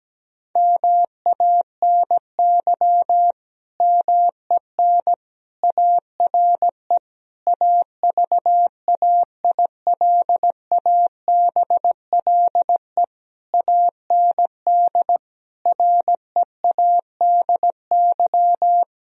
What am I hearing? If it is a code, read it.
Morse code, heard as a keyed tone, about 17 wpm, 700 Hz. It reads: MANY MEN ARE AVAILABLE AND READY
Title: Sentences from Top 1000 Words - Review Part 3 - 17wpm